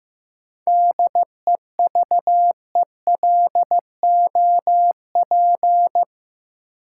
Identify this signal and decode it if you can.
Morse code, heard as a keyed tone, about 15 wpm, 700 Hz. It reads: DEVELOP
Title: Sentences from Top 1000 Words - Review Part 2 - New Word Drill - 15wpm